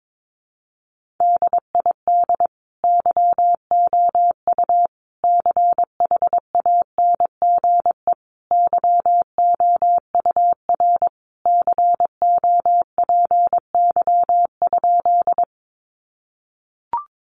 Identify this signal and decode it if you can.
Morse code, heard as a keyed tone, about 22 wpm, 700 Hz. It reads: DID YOU CHANGE YOUR COPY?
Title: Sentences from Top 1400 Words 22wpm